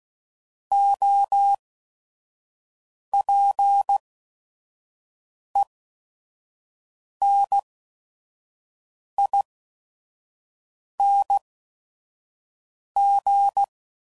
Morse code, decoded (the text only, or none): OPENING